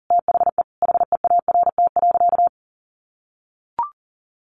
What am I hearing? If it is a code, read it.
Morse code, heard as a keyed tone, about 40 wpm, 700 Hz. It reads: THE HEART.
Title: Sentences from Top 1200 Words 40wpm